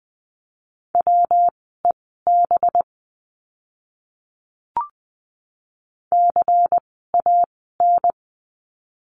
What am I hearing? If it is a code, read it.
Morse code, heard as a keyed tone, about 20 wpm, spaced as for 15 wpm, 700 Hz. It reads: WEB CAN
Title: 185 Lesson 30f - Letter b - Words 20wpm